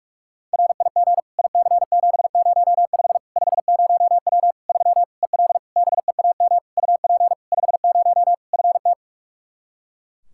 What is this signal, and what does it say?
Morse code, heard as a keyed tone, about 41 wpm, 700 Hz. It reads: RIG IC705 50W 3 EL BEAM UP 50 FT
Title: Practice QSO - Exchange 2 - Info - 45wpm